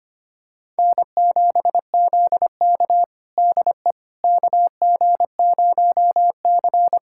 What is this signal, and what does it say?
Morse code, heard as a keyed tone, about 25 wpm, 700 Hz. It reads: N7ZK DE KG0C
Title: USA Callsign DE Callsign - 4 Characters - 25wpm